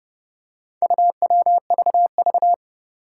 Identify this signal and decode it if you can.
Morse code, heard as a keyed tone, about 30 wpm, 700 Hz. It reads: UWVV
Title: Four Challenging Letter-Number Combinations 30wpm